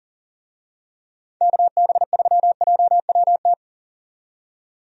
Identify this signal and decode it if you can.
Morse code, heard as a keyed tone, about 40 wpm, 700 Hz. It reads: KB3JWT
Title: USA Call Signs - 6 Characters - 40wpm